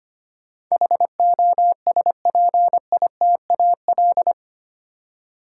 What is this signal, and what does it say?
Morse code, heard as a keyed tone, about 25 wpm, 700 Hz. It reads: HOSPITAL
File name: 8 Letter Words - First 3 Letters Given - 25wpm